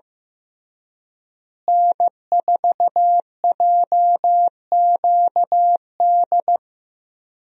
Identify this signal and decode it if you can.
Morse code, heard as a keyed tone, about 16 wpm, 700 Hz. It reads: N4JQD
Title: USA Call Signs - 5 Characters - Repeated 3x - 15wpm